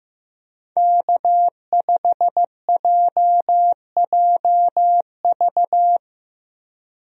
Morse code, heard as a keyed tone, about 15 wpm, 700 Hz. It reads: K5JJV
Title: USA Call Signs - 5 Characters - Repeated 2x - 15wpm